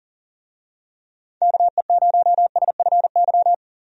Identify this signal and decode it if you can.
Morse code, heard as a keyed tone, about 37 wpm, 700 Hz. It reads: KE0SFY
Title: USA Call Signs - 6 Characters - 40wpm